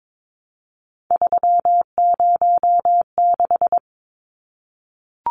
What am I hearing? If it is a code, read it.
Morse code, heard as a keyed tone, about 22 wpm, 700 Hz. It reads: 306
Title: Numbers - 3-Digit 22wpm